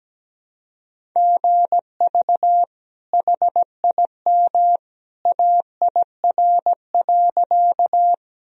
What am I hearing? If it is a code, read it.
Morse code, heard as a keyed tone, about 17 wpm, 700 Hz. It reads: GV HIM AIR.
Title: Sentences from Top 200 Words and Abbreviations 17wpm